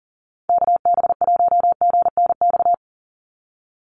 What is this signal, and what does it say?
Morse code, heard as a keyed tone, about 38 wpm, 700 Hz. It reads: KB1GNX
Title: USA Call Signs 40wpm